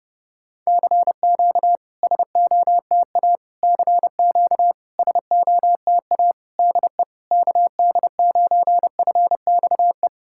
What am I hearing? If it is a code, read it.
Morse code, heard as a keyed tone, about 30 wpm, 690 Hz. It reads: CQ SOTA CQ SOTA DE KD9FXE
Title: SOTA QSO Practice 30wpm